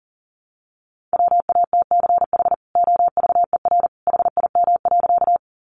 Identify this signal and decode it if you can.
Morse code, heard as a keyed tone, about 40 wpm, 700 Hz. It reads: WATCH OVER HIM.
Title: Sentences from Top 300 Words 40wpm